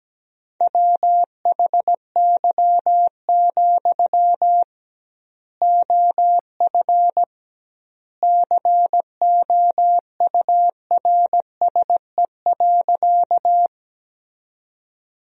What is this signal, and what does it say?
Morse code, heard as a keyed tone, about 17 wpm, 700 Hz. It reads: WHY, OF COURSE.
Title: 2x Word-Spacing - Sentences from Top 500 Words - 17wpm